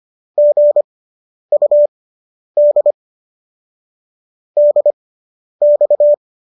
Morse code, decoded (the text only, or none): GUD DX